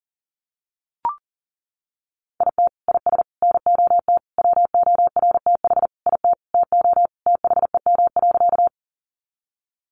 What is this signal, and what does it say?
Morse code, heard as a keyed tone, about 40 wpm, 700 Hz. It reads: IT IS NOT WORTH IT TO THEM.
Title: Sentences from Top 800 Words 40wpm